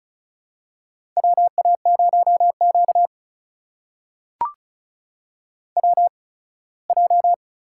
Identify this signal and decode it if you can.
Morse code, heard as a keyed tone, about 36 wpm, spaced as for 21 wpm, 700 Hz. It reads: WA0Q WJ
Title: USA Call Signs - 4 Characters - 8x Character Spacing - 35wpm